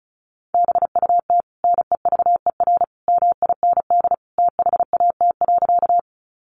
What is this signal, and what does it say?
Morse code, heard as a keyed tone, about 34 wpm, 700 Hz. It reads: BUT NEVER MIND THAT.
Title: Sentences from Top 500 Words 35wpm